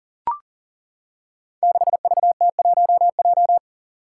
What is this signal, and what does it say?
Morse code, heard as a keyed tone, about 40 wpm, 700 Hz. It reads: 6VT1J
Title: Five Challenging Letter-Number Combinations 40wpm